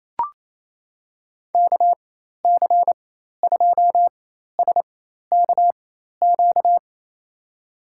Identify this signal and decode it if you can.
Morse code, heard as a keyed tone, about 28 wpm, spaced as for 13 wpm, 700 Hz. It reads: KC2SKQ
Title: USA Call Signs - 4x Character Spacing - 28wpm